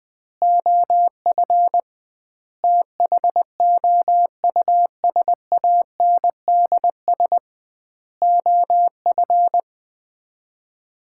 Morse code, heard as a keyed tone, about 20 wpm, 700 Hz. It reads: OF THOUSANDS OF
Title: 2x Word-Spacing - Sets of 4 Words - 20wpm